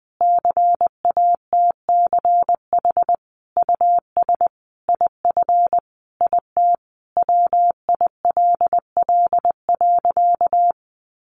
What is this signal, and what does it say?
Morse code, heard as a keyed tone, about 20 wpm, 700 Hz. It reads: CATCH US IF IT WILL.
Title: Sentences from Top 1500 Words 20wpm